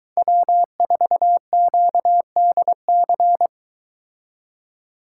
Morse code, heard as a keyed tone, about 23 wpm, 700 Hz. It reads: W4QDC